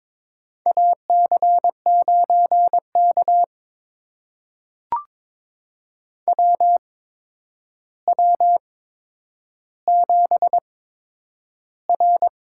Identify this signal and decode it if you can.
Morse code, heard as a keyed tone, about 22 wpm, spaced as for 9 wpm, 700 Hz. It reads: AC9K WW7R
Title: USA Call Signs - 8x Character Spacing - 22wpm